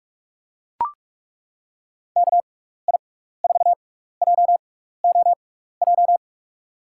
Morse code, heard as a keyed tone, about 41 wpm, spaced as for 15 wpm, 700 Hz. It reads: KI4JOJ
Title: USA Call Signs - 6x Character Spacing - 45wpm